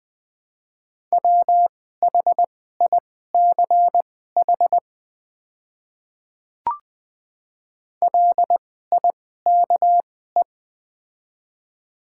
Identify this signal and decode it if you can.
Morse code, heard as a keyed tone, about 20 wpm, spaced as for 14 wpm, 690 Hz. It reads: WHICH LIKE
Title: 2x Character Spacing - Top 100 Words - 20wpm